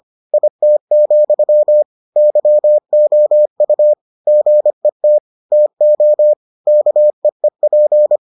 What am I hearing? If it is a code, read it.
Morse code, heard as a keyed tone, about 25 wpm, 600 Hz. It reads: IT, YOU GET TO KEEP